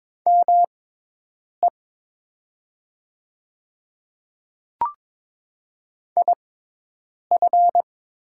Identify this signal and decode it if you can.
Morse code, heard as a keyed tone, about 22 wpm, spaced as for 8 wpm, 700 Hz. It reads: ME IF
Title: 6x Character Spacing - Top 200 Words - 22wpm